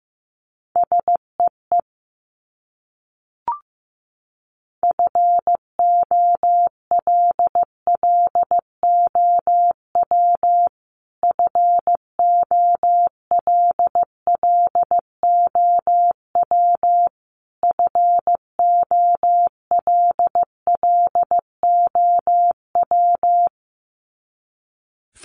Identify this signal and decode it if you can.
Morse code, heard as a keyed tone, about 15 wpm, 700 Hz. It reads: SEE FOLLOW FOLLOW FOLLOW
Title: Top 500 Words Repeated 15wpm